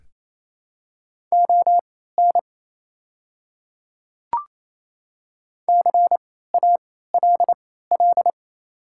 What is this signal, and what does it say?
Morse code, heard as a keyed tone, about 28 wpm, spaced as for 16 wpm, 700 Hz. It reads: ON CALL
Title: 3x Character Spacing - Top 100 Words - 28wpm